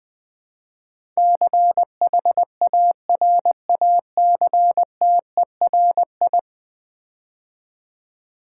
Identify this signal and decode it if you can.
Morse code, heard as a keyed tone, about 20 wpm, 700 Hz. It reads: CHARACTERI
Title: Build A Word - 5 Letters At A Time - 12-Letter Words - 20wpm